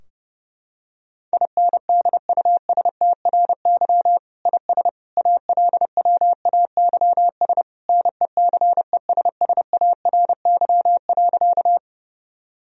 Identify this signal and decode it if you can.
Morse code, heard as a keyed tone, about 30 wpm, 700 Hz. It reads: INDUSTRY IS ALWAYS NECESSARY.
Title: Sentences from Top 1000 Words - Review Part 4 - 30wpm